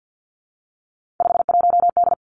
Morse code, heard as a keyed tone, about 50 wpm, 700 Hz. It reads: 51D
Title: Three Challenging Letter-Number Combinations 50wpm